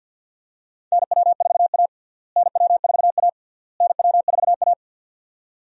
Morse code, heard as a keyed tone, about 48 wpm, 700 Hz. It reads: NW4A NW4A NW4A
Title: USA Call Signs - 4 Characters - Repeated 3x - 50wpm